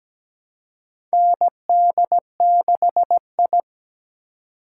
Morse code, heard as a keyed tone, about 17 wpm, 700 Hz. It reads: ND6I
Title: USA Call Signs - 4 Characters - 2x Character Spacing - 17wpm